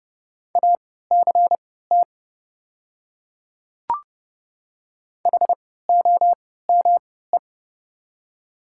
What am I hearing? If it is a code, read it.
Morse code, heard as a keyed tone, about 30 wpm, spaced as for 17 wpm, 700 Hz. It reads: ACT HOME
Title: 3x Character Spacing - Top 200 Words - 30wpm